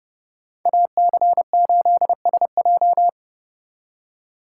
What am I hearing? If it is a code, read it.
Morse code, heard as a keyed tone, about 39 wpm, 700 Hz. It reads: AC8SJ